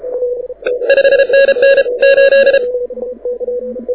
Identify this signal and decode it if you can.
Morse code, heard as a keyed tone, about 33 wpm, 1650 Hz, with radio noise. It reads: 5NN 8
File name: audio zawody / IARU 2015 CW SN7Q